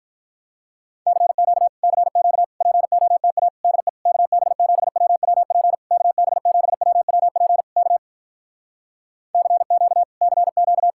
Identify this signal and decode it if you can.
Morse code, heard as a keyed tone, about 47 wpm, 700 Hz. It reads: CQ CQ POTA DE KB7WWP KB7WWP K CQ CQ